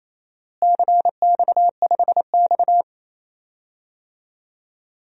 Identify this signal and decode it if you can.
Morse code, heard as a keyed tone, about 28 wpm, 700 Hz. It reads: CX5X
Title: DXCC-100 Call Signs - 4 Characters - Repeated 2x - 28wpm